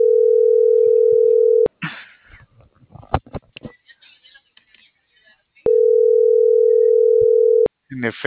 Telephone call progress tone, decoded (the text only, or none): ringback tone